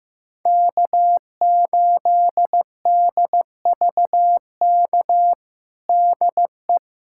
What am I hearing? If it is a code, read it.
Morse code, heard as a keyed tone, about 16 wpm, 700 Hz. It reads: K8DVK DE
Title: USA Callsign DE Callsign 15wpm